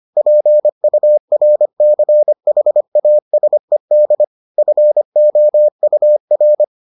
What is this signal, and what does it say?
Morse code, heard as a keyed tone, about 25 wpm, 600 Hz. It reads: PURCHASED FOUR